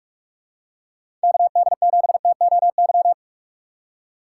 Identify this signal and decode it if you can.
Morse code, heard as a keyed tone, about 43 wpm, 700 Hz. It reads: KD7TOY
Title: USA Call Signs - 6x Character Spacing - 45wpm